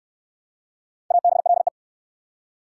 Morse code, heard as a keyed tone, about 56 wpm, 700 Hz. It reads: ABLE